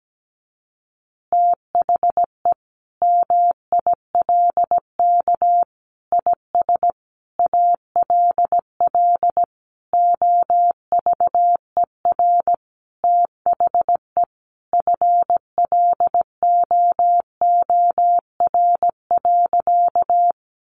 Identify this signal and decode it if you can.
Morse code, heard as a keyed tone, about 17 wpm, 700 Hz. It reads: THE MILK IS ALL OVER THE FLOOR.
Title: Additional Easy Phrases 17wpm